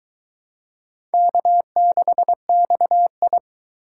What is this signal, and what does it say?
Morse code, heard as a keyed tone, about 23 wpm, 700 Hz. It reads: K6XI